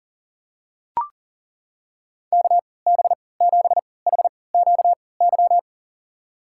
Morse code, heard as a keyed tone, about 40 wpm, spaced as for 23 wpm, 700 Hz. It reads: KB7HQY
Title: USA Call Signs - 6 Characters - 3x Character Spacing - 40wpm